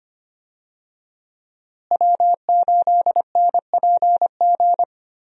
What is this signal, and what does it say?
Morse code, heard as a keyed tone, about 25 wpm, 700 Hz. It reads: W8NPG
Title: USA Call Signs - 5 Characters - Repeated 3x - 25wpm